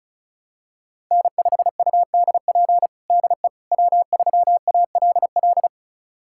Morse code, heard as a keyed tone, about 36 wpm, 700 Hz. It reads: N5UDP DE W3ALL
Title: USA Callsign DE Callsign - 5 Characters - 35wpm